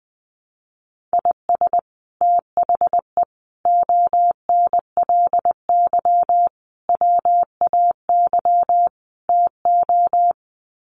Morse code, heard as a keyed tone, about 20 wpm, 700 Hz. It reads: IS THE ONLY WAY TO
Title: Sets of 5 Words Encore 20wpm